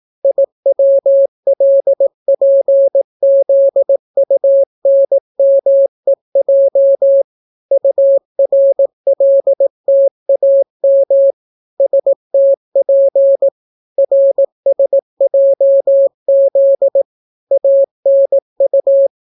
Morse code, heard as a keyed tone, about 18 wpm, 550 Hz. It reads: IWLPZUNMEJ URLTAM STP RSJZ ANU